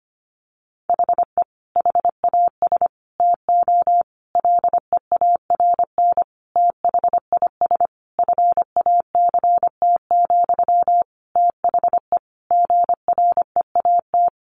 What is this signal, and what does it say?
Morse code, heard as a keyed tone, about 25 wpm, 700 Hz. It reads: HE HAS TO LEARN THIS FACT, THE GREAT
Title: Sentences from Top 1200 Words 25wpm